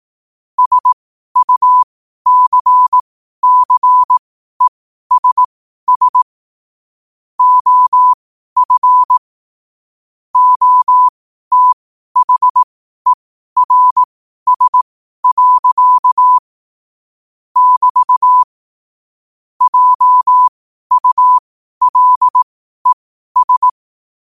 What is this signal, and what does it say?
Morse code, heard as a keyed tone, about 18 wpm, spaced as for 12 wpm, 1000 Hz. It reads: SUCCESS OF OTHERS. <BT> JULES
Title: 12 WPM morse code quotes for Thu, 29 Feb 2024